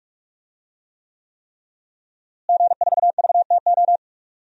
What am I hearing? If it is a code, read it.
Morse code, heard as a keyed tone, about 44 wpm, 700 Hz. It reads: G4VTO
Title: DXCC-100 Call Signs - 5 Characters - Repeated 3x - 45wpm